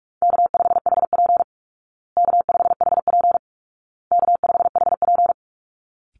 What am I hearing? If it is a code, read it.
Morse code, heard as a keyed tone, about 44 wpm, 700 Hz. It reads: K5HP K5HP K5HP
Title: Call Signs Repeated 2x - 45wpm